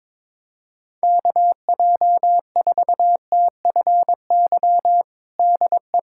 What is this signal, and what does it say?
Morse code, heard as a keyed tone, about 22 wpm, 700 Hz. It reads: KJ4TFY DE
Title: USA Callsign DE Callsign 22wpm